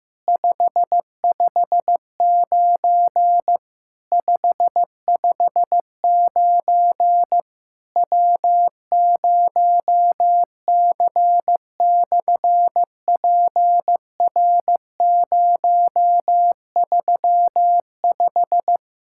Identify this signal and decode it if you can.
Morse code, heard as a keyed tone, about 15 wpm, 700 Hz. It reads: 559 559 W0C/PR035
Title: SOTA QSO Practice 15wpm